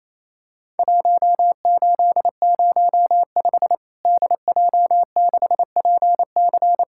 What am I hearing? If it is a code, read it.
Morse code, heard as a keyed tone, about 28 wpm, 700 Hz. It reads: 1805 DJ6PC
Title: Worked All Europe DX Contest - QTC Exchange - 28wpm